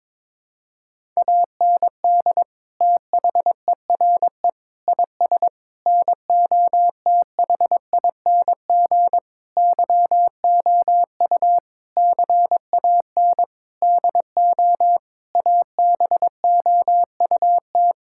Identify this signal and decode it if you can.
Morse code, heard as a keyed tone, about 22 wpm, 700 Hz. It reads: AND THERE IS NOTHING YOU CAN DO ABOUT
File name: practice / 22wpm / Sentences with 8-Word Phrases - 22wpm